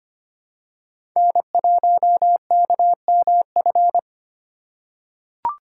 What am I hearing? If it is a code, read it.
Morse code, heard as a keyed tone, about 25 wpm, 700 Hz. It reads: N1KMF
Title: USA Call Signs - 2x Character Spacing - 25wpm